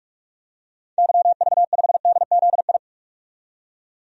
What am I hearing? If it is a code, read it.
Morse code, heard as a keyed tone, about 46 wpm, 700 Hz. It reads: YV5DZI
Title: DXCC-100 Call Signs - 6 Characters - 45wpm